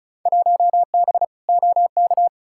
Morse code, heard as a keyed tone, about 34 wpm, 700 Hz. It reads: 1B OK
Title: Field Day Exchanges Encore - 35wpm